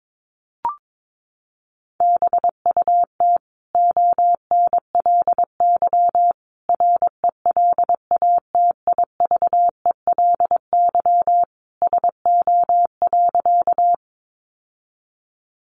Morse code, heard as a keyed tone, about 22 wpm, 700 Hz. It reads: BUT ONLY RELATIVELY SO.
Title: Sentences from Top 1100 Words 22wpm